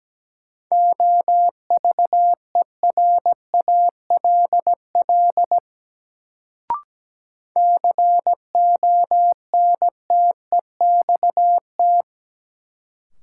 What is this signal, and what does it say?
Morse code, heard as a keyed tone, about 17 wpm, 700 Hz. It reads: OVERALL CONTEXT
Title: Sentences from Top 1000 Words - Review Part 7 - New Word Drill - 17wpm